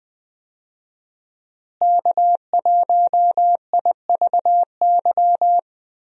Morse code, heard as a keyed tone, about 20 wpm, 700 Hz. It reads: K1IVY
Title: USA Call Signs - 5 Characters - Repeated 2x - 20wpm